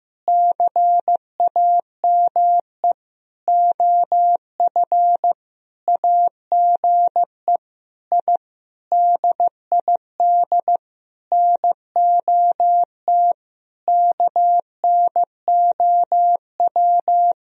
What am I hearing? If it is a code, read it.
Morse code, heard as a keyed tone, about 15 wpm, 700 Hz. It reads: CAME OF AGE I DID NOT KNOW
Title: Sentences from Top 1000 Words - Review Part 3 - 15wpm